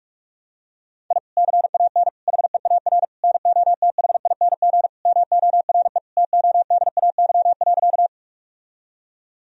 Morse code, heard as a keyed tone, about 44 wpm, 700 Hz. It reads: I CAN HEAR NOTHING MORE TODAY.
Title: Sentences from Top 1000 Words - Review Part 4 - 45wpm